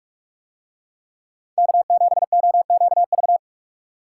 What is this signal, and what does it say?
Morse code, heard as a keyed tone, about 44 wpm, 700 Hz. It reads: K7OQV